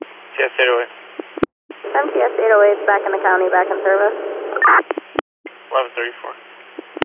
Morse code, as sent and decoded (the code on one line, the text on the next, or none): none